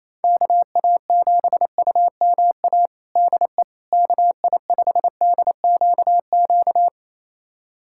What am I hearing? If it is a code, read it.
Morse code, heard as a keyed tone, about 28 wpm, 700 Hz. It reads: KA7UMA DE KI5DQQ
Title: USA Callsign DE Callsign - 6 Characters - 28wpm